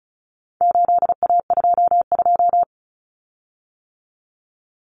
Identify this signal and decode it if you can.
Morse code, heard as a keyed tone, about 34 wpm, 700 Hz. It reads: 8A22
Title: Common Cut Numbers - 4-Digit 35wpm